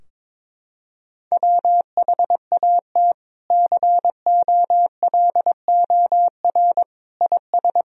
Morse code, heard as a keyed tone, about 21 wpm, 700 Hz. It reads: WHAT COLOR IS
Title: Sentences from Top 1000 Words - Review Part 2 - 22wpm